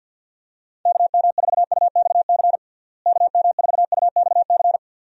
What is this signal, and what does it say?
Morse code, heard as a keyed tone, about 50 wpm, 700 Hz. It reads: KM4UXC KM4UXC
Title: USA Call Signs - 6 Characters - Repeated 2x - 50wpm